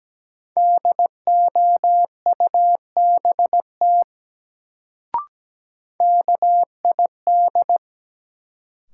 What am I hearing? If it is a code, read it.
Morse code, heard as a keyed tone, about 17 wpm, 700 Hz. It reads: DOUBT KID
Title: Sets of 5 Words - New Word Drill - 17wpm